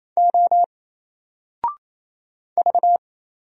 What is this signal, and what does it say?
Morse code, heard as a keyed tone, about 27 wpm, 700 Hz. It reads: O V
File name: Single-Letter-Number-28wpm